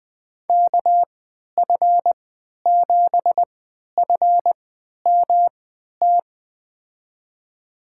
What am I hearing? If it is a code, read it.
Morse code, heard as a keyed tone, about 21 wpm, spaced as for 11 wpm, 700 Hz. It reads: KF7FMT
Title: USA Call Signs - 6 Characters - 3x Character Spacing - 20wpm